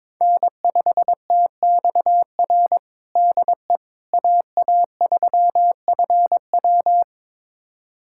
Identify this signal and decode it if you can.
Morse code, heard as a keyed tone, about 22 wpm, 700 Hz. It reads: N5TXR DE AA3FW
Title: USA Callsign DE Callsign - 5 Characters - 22wpm